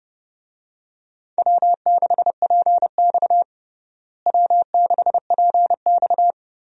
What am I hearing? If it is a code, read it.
Morse code, heard as a keyed tone, about 30 wpm, 700 Hz. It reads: W6PX W6PX
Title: USA Call Signs - 4 Characters - Repeated 3x - 30wpm